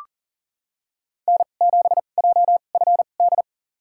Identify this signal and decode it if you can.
Morse code, heard as a keyed tone, about 40 wpm, spaced as for 29 wpm, 700 Hz. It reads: N7JFD
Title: USA Call Signs - 5 Characters - 2x Character Spacing - 40wpm